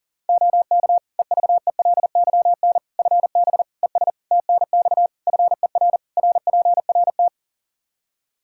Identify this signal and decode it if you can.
Morse code, heard as a keyed tone, about 40 wpm, 700 Hz. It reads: OK EVELYN FB ES TNX FER RPRT
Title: Practice QSO - Exchange 2 - Info - 40wpm